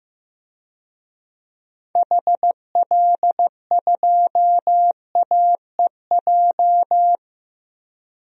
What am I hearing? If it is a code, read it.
Morse code, heard as a keyed tone, about 15 wpm, 700 Hz. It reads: HL2AEJ
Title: DXCC-100 Call Signs - 6 Characters - 15wpm